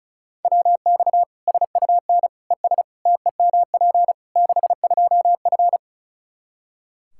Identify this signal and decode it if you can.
Morse code, heard as a keyed tone, about 36 wpm, 700 Hz. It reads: WX SUN ES TEMP 62F
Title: Practice QSO - Exchange 2 - Info - 35wpm